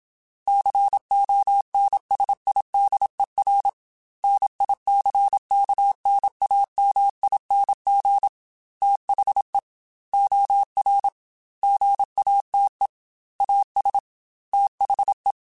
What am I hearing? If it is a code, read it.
Morse code, heard as a keyed tone, about 27 wpm, 800 Hz. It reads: CONSIDER NICKNAMING THE OR GATE AS THE
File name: Technical Subjects / Technical Subjects-mp3s / OR Gate @25